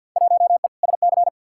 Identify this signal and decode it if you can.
Morse code, heard as a keyed tone, about 50 wpm, 700 Hz. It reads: 1E SC
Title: Field Day Exchanges - 50wpm